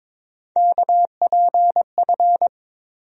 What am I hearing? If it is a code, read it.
Morse code, heard as a keyed tone, about 22 wpm, 700 Hz. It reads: KPF
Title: USA Call Sign Postfixes - 3 Letters 22wpm